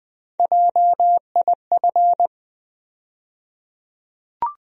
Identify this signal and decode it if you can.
Morse code, heard as a keyed tone, about 20 wpm, 700 Hz. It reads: JIF
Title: USA Call Sign Postfixes - 3 Letters 20wpm